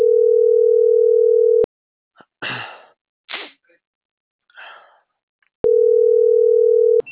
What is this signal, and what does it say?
Ringback tone (North America), 440 + 480 Hz.